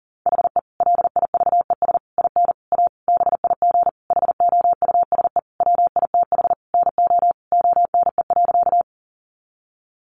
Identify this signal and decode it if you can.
Morse code, heard as a keyed tone, about 40 wpm, 700 Hz. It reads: HE LIVES IN A BIG HOUSE WITH NO ONE.
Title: Additional Easy Phrases 40wpm